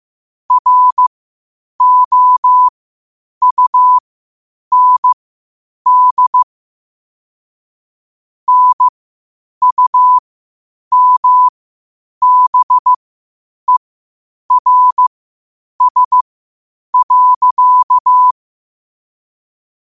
Morse code, heard as a keyed tone, about 15 wpm, spaced as for 8 wpm, 1000 Hz. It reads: ROUND NUMBERS.
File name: qotdfiles / 8wpm